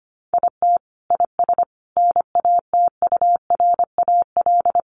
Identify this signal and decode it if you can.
Morse code, heard as a keyed tone, about 25 wpm, 700 Hz. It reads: IT IS NATURAL
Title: Sentences from Top 700 Words 25wpm